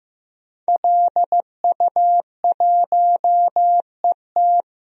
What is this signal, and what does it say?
Morse code, heard as a keyed tone, about 15 wpm, 700 Hz. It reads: LU1ET
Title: DXCC-100 Call Signs - 5 Characters - 15wpm